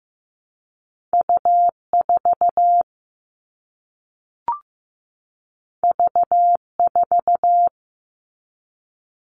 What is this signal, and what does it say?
Morse code, heard as a keyed tone, about 15 wpm, 700 Hz. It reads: U4 V4
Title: All Cut Numbers - 2-Digit 15wpm